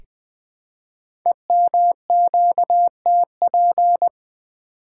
Morse code, heard as a keyed tone, about 20 wpm, 700 Hz. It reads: EMQTP
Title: Code Groups - 5 Letters - 20wpm